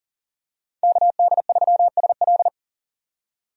Morse code, heard as a keyed tone, about 40 wpm, 700 Hz. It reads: KD3SL
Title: USA Call Signs - 5 Characters - 40wpm